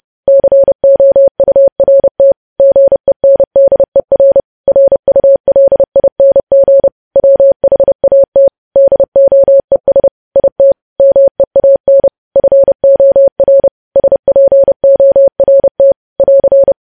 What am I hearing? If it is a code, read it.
Morse code, heard as a keyed tone, about 30 wpm, 550 Hz. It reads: COURT GENDER RULING WHAT DOES IT MEAN FOR SPORT <AR>
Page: news headlines in morse code at 30 wpm